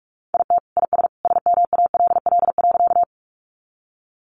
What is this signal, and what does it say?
Morse code, heard as a keyed tone, about 44 wpm, 700 Hz. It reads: IT IS SMALL.
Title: Easy Phrases 45wpm